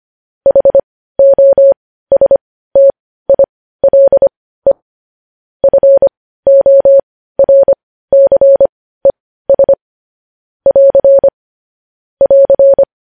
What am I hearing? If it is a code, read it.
Morse code, heard as a keyed tone, about 25 wpm, spaced as for 15 wpm, 550 Hz. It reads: HOSTILE FORCES <AR> <AR>